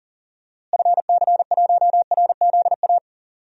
Morse code, heard as a keyed tone, about 38 wpm, 700 Hz. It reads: FC1RZA
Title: DXCC-100 Call Signs - 6 Characters - Repeated 2x - 40wpm